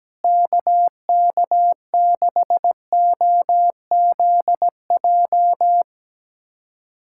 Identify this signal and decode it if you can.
Morse code, heard as a keyed tone, about 17 wpm, 700 Hz. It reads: KK6OZJ
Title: USA Callsign DE Callsign 17wpm